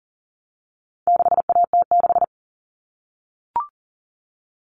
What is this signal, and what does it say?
Morse code, heard as a keyed tone, about 40 wpm, 700 Hz. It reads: 6AT6
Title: All Cut Numbers - 4-Digit 40wpm